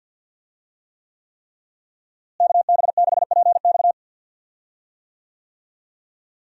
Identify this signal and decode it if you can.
Morse code, heard as a keyed tone, about 47 wpm, 700 Hz. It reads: KB6PX
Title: USA Call Signs - 5 Characters - Repeated 3x - 50wpm